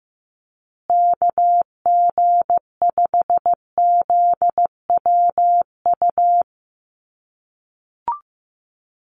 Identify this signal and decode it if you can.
Morse code, heard as a keyed tone, about 15 wpm, 700 Hz. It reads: KG5ZWU
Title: USA Call Signs Repeated 3x 15wpm